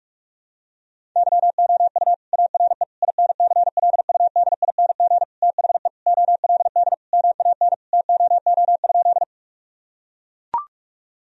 Morse code, heard as a keyed tone, about 41 wpm, 700 Hz. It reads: YOU ARE INCLUDING THE OLD MAN TOO?
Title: Sentences from Top 1000 Words - Review Part 4 - 45wpm